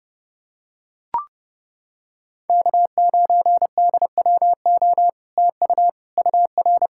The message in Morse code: K9DWO TU UR

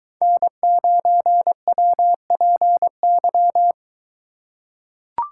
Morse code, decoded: N9WPY

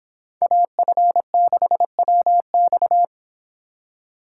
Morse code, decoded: AF6WX

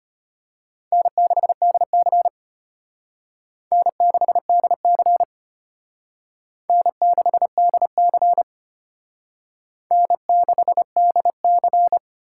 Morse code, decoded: N6DC N6DC N6DC N6DC

